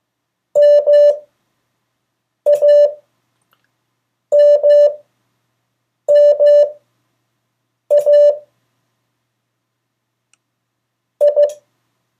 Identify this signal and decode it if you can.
Morse code, heard as a keyed tone, about 15 wpm, spaced as for 6 wpm, 580 Hz. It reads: MAMMA I
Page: CW Class Review – Delaware Valley Radio Association